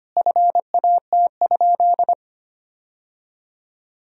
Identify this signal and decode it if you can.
Morse code, heard as a keyed tone, about 25 wpm, 700 Hz. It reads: FAT?
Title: Sentences from Top 1000 Words - Review Part 4 - 25wpm